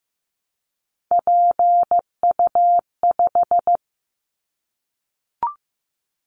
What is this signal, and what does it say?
Morse code, heard as a keyed tone, about 15 wpm, 700 Hz. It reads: PU5
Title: Callsign Prefixes for Top 50 DXCC - 15wpm